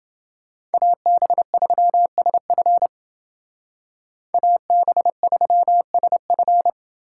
Morse code, heard as a keyed tone, about 28 wpm, 700 Hz. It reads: AB3SF AB3SF